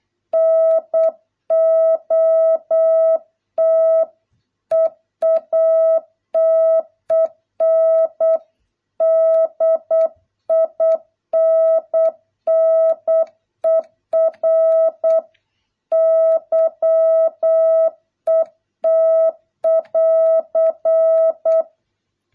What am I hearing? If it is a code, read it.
Morse code, heard as a keyed tone, about 8 wpm, 650 Hz. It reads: NOT EATEN DINNER YET <AR>